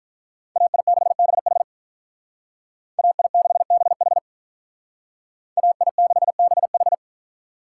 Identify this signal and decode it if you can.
Morse code, heard as a keyed tone, about 45 wpm, 700 Hz. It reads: AI6BH AI6BH AI6BH